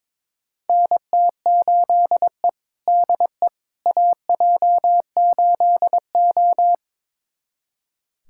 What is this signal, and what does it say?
Morse code, heard as a keyed tone, about 22 wpm, 700 Hz. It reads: NT8E DE AJ8O
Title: USA Callsign DE Callsign - 4 Characters - 22wpm